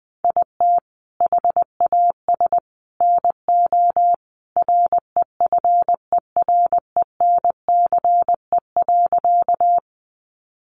Morse code, heard as a keyed tone, about 20 wpm, 700 Hz. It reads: IT HAS NO REFERENCE.
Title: Sentences from Top 1200 Words 20wpm